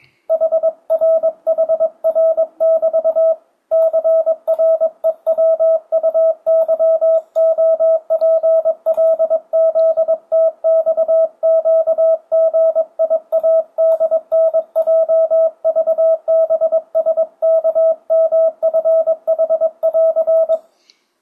Morse code, heard as a keyed tone, about 22 wpm, 650 Hz. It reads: HRHR<BT>CREWUYOPLZTXQGIADNJVBSKMFH<AR>
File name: 2025cw / speedup / 22wpm